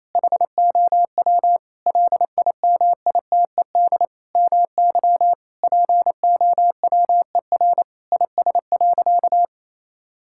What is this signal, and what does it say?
Morse code, heard as a keyed tone, about 27 wpm, 700 Hz. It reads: HOW LIMITED MY POWER IS.
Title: Sentences from Top 1000 Words - Review Part 6 - 28wpm